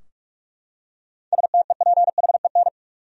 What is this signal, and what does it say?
Morse code, heard as a keyed tone, about 46 wpm, 700 Hz. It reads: STEPHEN
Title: USA Common Names - 45wpm